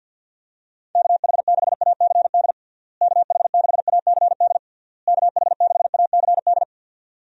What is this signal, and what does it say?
Morse code, heard as a keyed tone, about 50 wpm, 700 Hz. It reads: KH6ACD KH6ACD KH6ACD
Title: USA Call Signs - 6 Characters - Repeated 3x - 50wpm